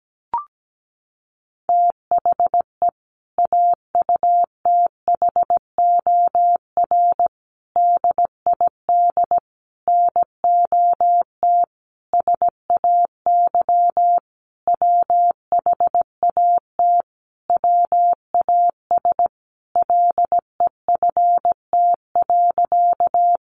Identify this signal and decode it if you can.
Morse code, heard as a keyed tone, about 17 wpm, 700 Hz. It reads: THE AUTHOR DID NOT SAY WHAT WAS LEFT.
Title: Sentences from Top 1200 Words 17wpm